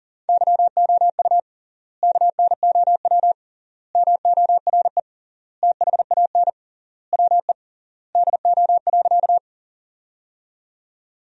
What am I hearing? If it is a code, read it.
Morse code, heard as a keyed tone, about 40 wpm, 700 Hz. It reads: YOU KNOW MORE THAN WE DO.
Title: 3x Word-Spacing - Sentences from Top 100 Words - 40wpm